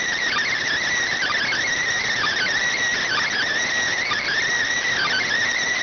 Emergency Alert System (SAME) tones, heard as harsh buzzing bursts. It is heard from 1.1 to 5.3 s.